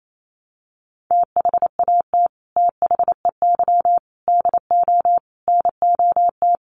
Morse code, heard as a keyed tone, about 28 wpm, 700 Hz. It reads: THAT THEY DO NOT
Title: Sets of 5 Words Encore2 28wpm